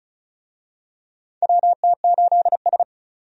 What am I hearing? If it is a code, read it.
Morse code, heard as a keyed tone, about 36 wpm, 700 Hz. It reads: WT8S
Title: USA Call Signs - 4 Characters - 35wpm